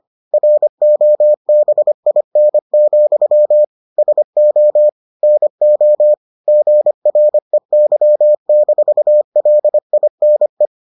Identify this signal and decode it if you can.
Morse code, heard as a keyed tone, about 25 wpm, 600 Hz. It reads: ROBIN, SO NO GREY-LINE